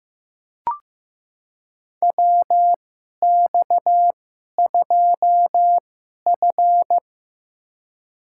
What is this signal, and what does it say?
Morse code, heard as a keyed tone, about 15 wpm, spaced as for 11 wpm, 700 Hz. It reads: WX2F
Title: USA Call Signs - 2x Character Spacing - 15wpm